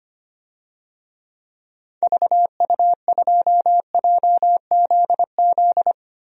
Morse code, heard as a keyed tone, about 25 wpm, 700 Hz. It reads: VU2JZZ
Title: DXCC-100 Call Signs - 6 Characters - 25wpm